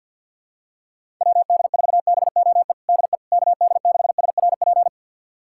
Morse code, heard as a keyed tone, about 50 wpm, 700 Hz. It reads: WD4BOE DE KD6SRP
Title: USA Callsign DE Callsign 50wpm